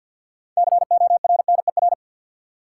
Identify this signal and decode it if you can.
Morse code, heard as a keyed tone, about 52 wpm, 700 Hz. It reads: CORNER
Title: Six Letter Words - 50wpm